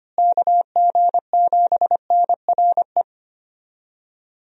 Morse code, heard as a keyed tone, about 25 wpm, 700 Hz. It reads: KG7NRE
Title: USA Call Signs - 6 Characters - Repeated 3x - 25wpm